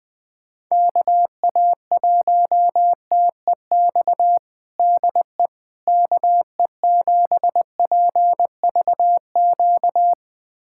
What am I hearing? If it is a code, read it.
Morse code, heard as a keyed tone, about 20 wpm, 700 Hz. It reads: KA1TEX DE KE7PVQ
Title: USA Callsign DE Callsign 20wpm